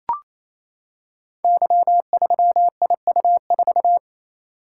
Morse code, heard as a keyed tone, about 28 wpm, 700 Hz. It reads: Y3IU4
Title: Code Groups - 5 Letter:Numbers - 28wpm